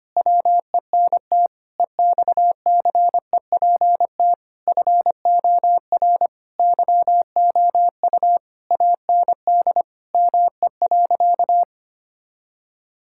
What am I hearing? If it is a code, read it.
Morse code, heard as a keyed tone, about 25 wpm, 700 Hz. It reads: WENT EXCEPT FOR YOU AND ME.
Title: Sentences from Top 1600 Words 25wpm